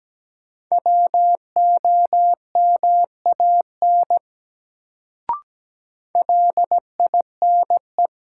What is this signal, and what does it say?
Morse code, heard as a keyed tone, about 17 wpm, 700 Hz. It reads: WOMAN LINE
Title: Sets of 5 Words - New Word Drill - 17wpm